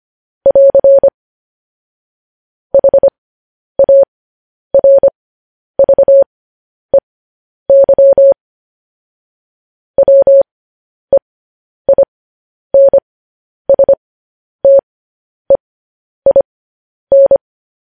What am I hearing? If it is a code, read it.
Morse code, heard as a keyed tone, about 25 wpm, spaced as for 10 wpm, 550 Hz. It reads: <AR> HARVEY WEINSTEIN
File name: podcast / News headlines-1x-25x10